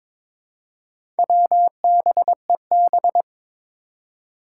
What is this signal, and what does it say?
Morse code, heard as a keyed tone, about 22 wpm, 700 Hz. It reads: WBEB
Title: Four Challenging Letter-Number Combinations 22wpm